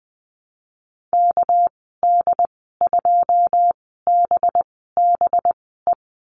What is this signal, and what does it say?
Morse code, heard as a keyed tone, about 20 wpm, spaced as for 15 wpm, 700 Hz. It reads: KD2BBE
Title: 213 Lesson 34g - Number 8 - Call Signs 20wpm